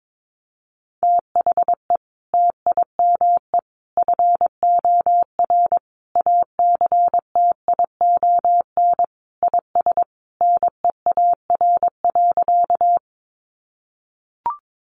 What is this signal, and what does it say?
Morse code, heard as a keyed tone, about 22 wpm, 700 Hz. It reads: THE TIME FOR ACTION IS NEAR.
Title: Sentences from Top 700 Words 22wpm